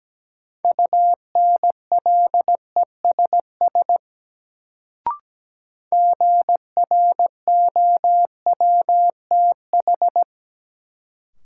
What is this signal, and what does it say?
Morse code, heard as a keyed tone, about 17 wpm, 700 Hz. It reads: UNLESS GROWTH
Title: Sentences from Top 1300 Words - New Word Drill Alt - 17wpm